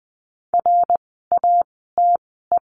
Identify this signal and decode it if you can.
Morse code, heard as a keyed tone, about 20 wpm, spaced as for 15 wpm, 700 Hz. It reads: RATE
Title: 075 Lesson 13f - Letter L - Words 20wpm